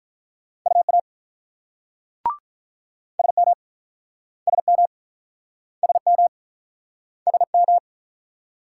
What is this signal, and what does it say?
Morse code, heard as a keyed tone, about 50 wpm, 700 Hz. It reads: UA SM SM SM SM